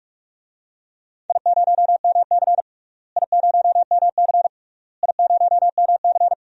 Morse code, heard as a keyed tone, about 42 wpm, 700 Hz. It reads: I0MC I0MC I0MC
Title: DXCC-100 Call Signs - 4 Characters - Repeated 3x - 45wpm